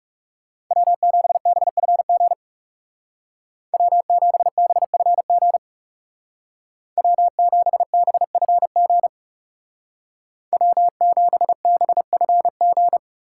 Morse code, heard as a keyed tone, about 36 wpm, 700 Hz. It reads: W7BFG W7BFG W7BFG W7BFG
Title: Speed Racing - USA 1x3 Call Signs 30wpm